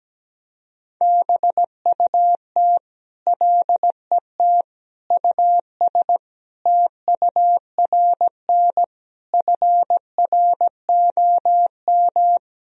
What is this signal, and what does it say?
Morse code, heard as a keyed tone, about 17 wpm, 700 Hz. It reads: BUT LET US TURN FROM
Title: Sentences from Top 1000 Words - Review Part 4 - 17wpm